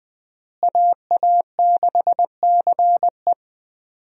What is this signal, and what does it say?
Morse code, heard as a keyed tone, about 20 wpm, 700 Hz. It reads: AA6CE
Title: USA Call Signs - 5 Characters - Repeated 3x - 20wpm